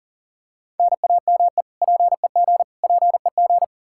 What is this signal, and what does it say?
Morse code, heard as a keyed tone, about 40 wpm, 700 Hz. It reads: NAME PEG PEG
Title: Practice QSO - Exchange 1 - Beginning - 40wpm